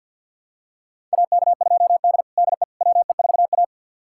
Morse code, heard as a keyed tone, about 52 wpm, 700 Hz. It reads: AK2D DE WE4A
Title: USA Callsign DE Callsign - 4 Characters - 50wpm